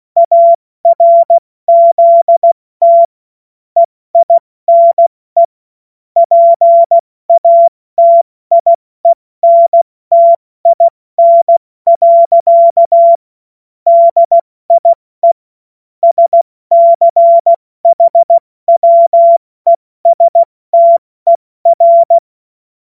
Morse code, heard as a keyed tone, about 16 wpm, 680 Hz. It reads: ARZT EINE PATIENTIN. DIE SCHWESTER
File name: Morsekurs / Witze / Witz 20. Juni 2023 70 F80